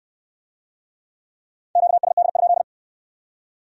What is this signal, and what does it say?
Morse code, heard as a keyed tone, about 64 wpm, 700 Hz. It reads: QSD?